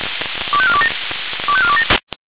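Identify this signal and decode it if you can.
Five-tone selective call (ZVEI), heard as quick runs of five tones, tones of about 70 ms. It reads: 25427 25427
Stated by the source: ZVEI five-tone selective call received on SDR.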